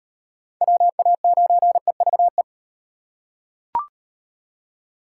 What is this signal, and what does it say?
Morse code, heard as a keyed tone, about 35 wpm, 700 Hz. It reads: WA9EVE